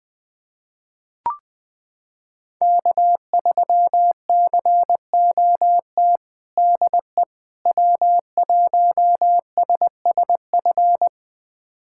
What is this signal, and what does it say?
Morse code, heard as a keyed tone, about 20 wpm, 700 Hz. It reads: K3COT DE W1SSF
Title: USA Callsign DE Callsign 20wpm